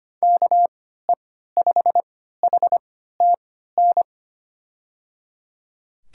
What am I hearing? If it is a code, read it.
Morse code, heard as a keyed tone, about 25 wpm, spaced as for 14 wpm, 700 Hz. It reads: KE5HTN
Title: USA Call Signs - 6 Characters - 3x Character Spacing - 25wpm